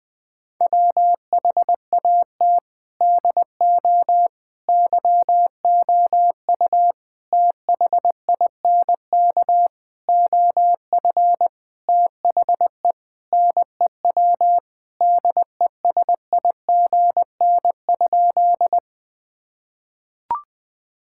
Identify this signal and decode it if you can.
Morse code, heard as a keyed tone, about 21 wpm, 700 Hz. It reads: WHAT DO YOU THINK OF THE NEW DESIGN?
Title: Sentences with 5-Word Phrases - 20wpm